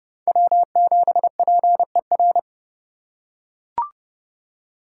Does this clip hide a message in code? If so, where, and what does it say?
Yes: Morse code from 0.3 to 2.4 s, heard as a keyed tone, about 27 wpm, 700 Hz. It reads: W7PER